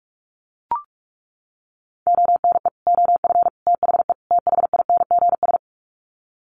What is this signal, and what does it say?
Morse code, heard as a keyed tone, about 44 wpm, 700 Hz. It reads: ONE OF THE THINGS
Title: Sets of 4 Words 45wpm